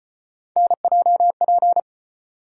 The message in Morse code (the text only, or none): NJP